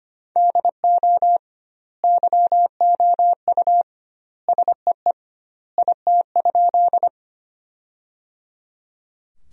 Morse code, heard as a keyed tone, about 25 wpm, 700 Hz. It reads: DO YOU SEE IT?
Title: 2x Word-Spacing - Sentences from Top 100 Words - 25wpm